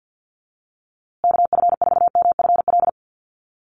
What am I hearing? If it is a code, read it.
Morse code, heard as a keyed tone, about 50 wpm, 700 Hz. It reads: KF4MFL
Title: USA Call Signs Repeated 2x 50wpm